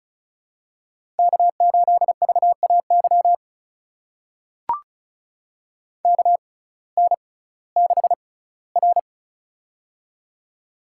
Morse code, heard as a keyed tone, about 36 wpm, spaced as for 19 wpm, 700 Hz. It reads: K8VAY KN6R
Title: USA Call Signs - 6x Character Spacing - 35wpm